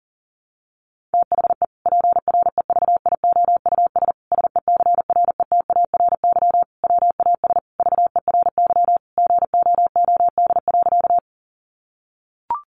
Morse code, heard as a keyed tone, about 37 wpm, 700 Hz. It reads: THE PREVIOUS SECRETARY WAS VERY GOOD.
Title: Sentences from Top 1500 Words 40wpm